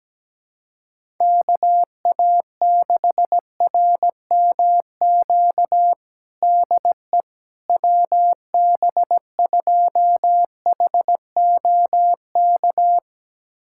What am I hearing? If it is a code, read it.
Morse code, heard as a keyed tone, about 17 wpm, 700 Hz. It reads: KA6RMQ DE WB2HOK
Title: USA Callsign DE Callsign - 6 Characters - 17wpm